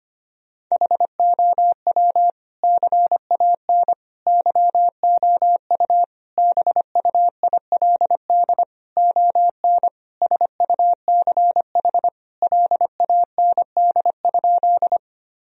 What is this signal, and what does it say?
Morse code, heard as a keyed tone, about 25 wpm, 700 Hz. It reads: HOW CAN YOU BUILD ON SUCH LAND?
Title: Sentences from Top 1000 Words - Review Part 1 - 25wpm